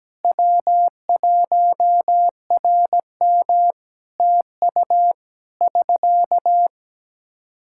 Morse code, heard as a keyed tone, about 17 wpm, 700 Hz. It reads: W1RM TU <SK>
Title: Practice QSO - Exchange 4 - Ending - 17wpm